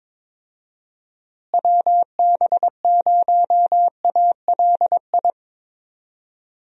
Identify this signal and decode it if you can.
Morse code, heard as a keyed tone, about 22 wpm, 700 Hz. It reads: WB0ALI
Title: USA Call Signs - 6 Characters - Repeated 2x - 22wpm